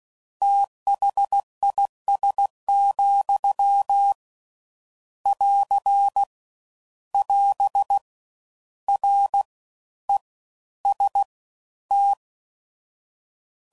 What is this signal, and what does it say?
Morse code, heard as a keyed tone, about 16 wpm, spaced as for 12 wpm, 800 Hz. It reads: THIS, <AR> <AS> REST